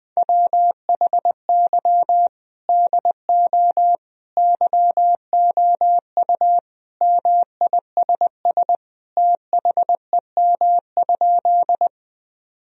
Morse code, heard as a keyed tone, about 21 wpm, 700 Hz. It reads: WHY DO YOU MISS THEM?
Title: Sentences from Top 1000 Words - Review Part 3 - 20wpm